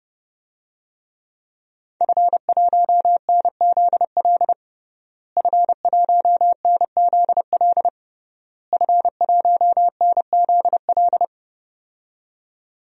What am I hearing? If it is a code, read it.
Morse code, heard as a keyed tone, about 32 wpm, 700 Hz. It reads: F1NZL F1NZL F1NZL
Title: DXCC-100 Call Signs - 5 Characters - Repeated 3x - 30wpm